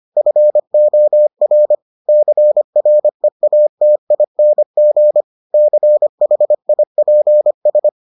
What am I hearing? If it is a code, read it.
Morse code, heard as a keyed tone, about 25 wpm, 600 Hz. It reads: FOR CREATING CHIPS